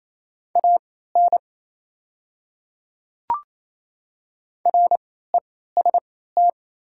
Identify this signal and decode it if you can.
Morse code, heard as a keyed tone, about 28 wpm, spaced as for 16 wpm, 690 Hz. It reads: AN REST
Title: 3x Character Spacing - Top 500 Words - 28wpm